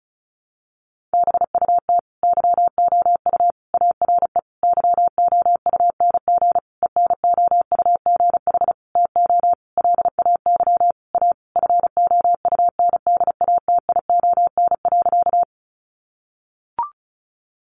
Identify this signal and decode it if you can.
Morse code, heard as a keyed tone, about 34 wpm, 700 Hz. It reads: BUT YOU ARE YOUNG ENOUGH TO LAY A FOUNDATION.
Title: Sentences from Top 1200 Words 35wpm